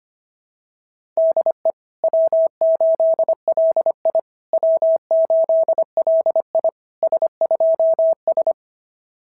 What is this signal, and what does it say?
Morse code, heard as a keyed tone, about 25 wpm, 660 Hz. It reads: DE W8LI W8LI S2S
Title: SOTA QSO Practice 25wpm